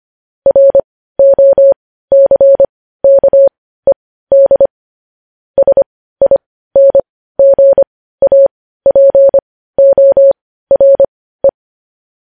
Morse code, heard as a keyed tone, about 25 wpm, spaced as for 15 wpm, 550 Hz. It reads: ROCKED SINGAPORE
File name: podcast / News headlines-1x-25x15